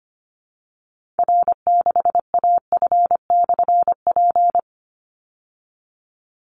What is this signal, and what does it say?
Morse code, heard as a keyed tone, about 25 wpm, 700 Hz. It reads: R6AF/P
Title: Call Signs Repeated 3x - 25wpm